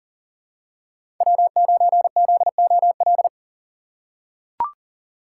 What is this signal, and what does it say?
Morse code, heard as a keyed tone, about 40 wpm, 700 Hz. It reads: W9ZOL
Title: USA Call Signs - 5 Characters - 6x Character Spacing - 40wpm